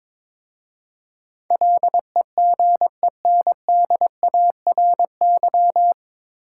Morse code, heard as a keyed tone, about 22 wpm, 700 Hz. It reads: LEGENDARY
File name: 9 Letter Words - First 3 Letters Given - 22wpm